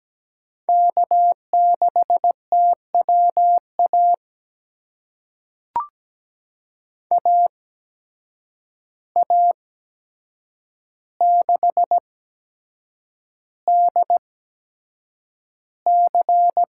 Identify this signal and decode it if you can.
Morse code, heard as a keyed tone, about 17 wpm, spaced as for 7 wpm, 700 Hz. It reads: K6TWA AA6DC
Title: USA Call Signs - 8x Character Spacing - 17wpm